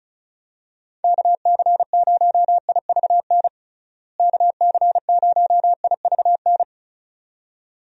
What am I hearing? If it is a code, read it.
Morse code, heard as a keyed tone, about 34 wpm, 700 Hz. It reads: KC0IVN KC0IVN
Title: USA Call Signs - 6 Characters - Repeated 2x - 35wpm